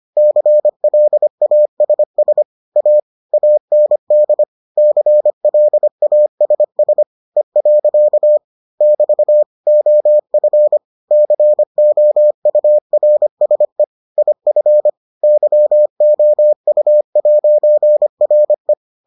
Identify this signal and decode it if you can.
Morse code, heard as a keyed tone, about 25 wpm, 600 Hz. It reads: CLASS A AND CLASS E. <BT> OF COURSE IF YOU'RE